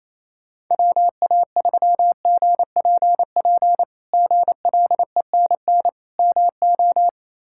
Morse code, heard as a keyed tone, about 28 wpm, 700 Hz. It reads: WA3GPP GLENN MO
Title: K1USN SST QSO Practice - 28wpm